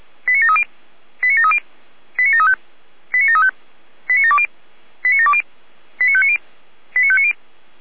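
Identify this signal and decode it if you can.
Five-tone selective call (ZVEI), heard as quick runs of five tones, tones of about 70 ms. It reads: 89730 89730 89735 89735 89720 89720 89590 89590
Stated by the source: Received by SDR as a ZVEI five-tone selective call.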